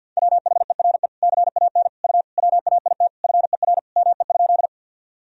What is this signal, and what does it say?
Morse code, heard as a keyed tone, about 51 wpm, 700 Hz. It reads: WHERE CAN U WAIT FER ME?
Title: Sentences from Top 500 Words and Abbreviations 50wpm